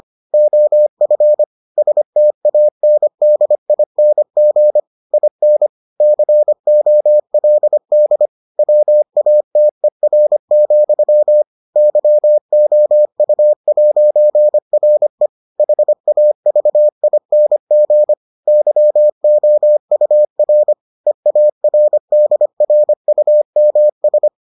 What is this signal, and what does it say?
Morse code, heard as a keyed tone, about 25 wpm, 600 Hz. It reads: OF STANDING IN COLD WATER, YOU'RE HAVING YOUR EARDRUMS